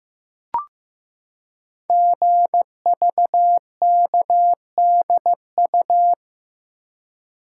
Morse code, heard as a keyed tone, about 15 wpm, 700 Hz. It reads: GVKDU
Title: Code Groups - 5 Letter:Numbers - 15wpm